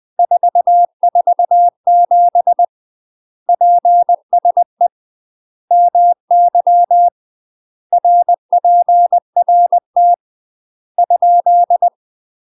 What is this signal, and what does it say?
Morse code, heard as a keyed tone, about 20 wpm, 700 Hz. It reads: 447 PSE MY RPRT ?
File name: UK MASTER QSOs TRAINING / 20WPM / UK MASTER QSOs 0032